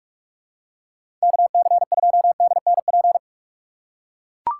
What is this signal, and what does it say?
Morse code, heard as a keyed tone, about 43 wpm, 700 Hz. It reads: KC2DNP
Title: USA Call Signs - 2x Character Spacing - 45wpm